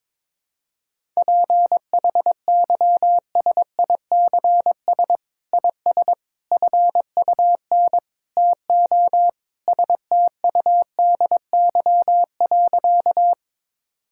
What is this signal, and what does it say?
Morse code, heard as a keyed tone, about 22 wpm, 700 Hz. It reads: PHYSICS IS FUN TO STUDY.
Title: Sentences from Top 1000 Words - Review Part 5 - 22wpm